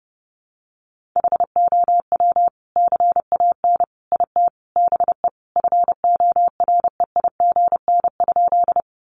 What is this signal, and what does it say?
Morse code, heard as a keyed tone, about 30 wpm, 700 Hz. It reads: HOW CAN IT BE FOREIGN?
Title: Sentences from Top 1400 Words 30wpm